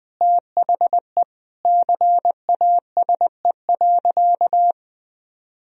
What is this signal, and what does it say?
Morse code, heard as a keyed tone, about 21 wpm, 700 Hz. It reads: THE CASE.
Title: Sentences from Top 1000 Words - Review Part 4 - 20wpm